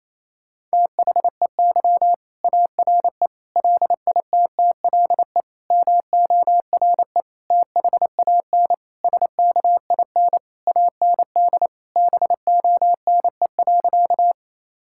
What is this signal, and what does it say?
Morse code, heard as a keyed tone, about 28 wpm, 700 Hz. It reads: THEY ARE LITTLE MORE THAN SKIN AND BONE.
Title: Sentences from Top 1600 Words 28wpm